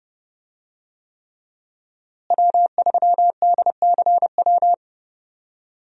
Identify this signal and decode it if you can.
Morse code, heard as a keyed tone, about 30 wpm, 700 Hz. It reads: W3DCW
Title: USA Call Signs - 5 Characters - Repeated 3x - 30wpm